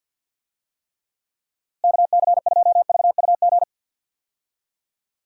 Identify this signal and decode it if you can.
Morse code, heard as a keyed tone, about 49 wpm, 700 Hz. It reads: KC2VUG
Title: USA Call Signs - 6 Characters - Repeated 3x - 50wpm